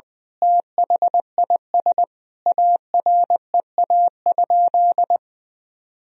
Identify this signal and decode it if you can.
Morse code, heard as a keyed tone, about 21 wpm, 700 Hz. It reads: THIS AREA?
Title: Sentences from Top 1000 Words - Review Part 3 - 20wpm